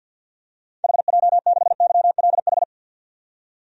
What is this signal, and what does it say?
Morse code, heard as a keyed tone, about 48 wpm, 700 Hz. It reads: HJ6YLH
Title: DXCC-100 Call Signs - 6 Characters - 50wpm